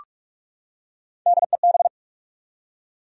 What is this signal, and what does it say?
Morse code, heard as a keyed tone, about 46 wpm, 700 Hz. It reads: DEB
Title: USA Common Names - Max 4 Letters - 45wpm